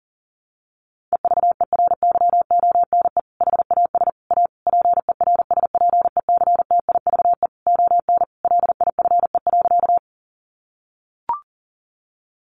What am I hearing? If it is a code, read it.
Morse code, heard as a keyed tone, about 40 wpm, 700 Hz. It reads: EVERYONE HAS A PERSPECTIVE ON LIFE.
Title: Sentences from Top 1200 Words 40wpm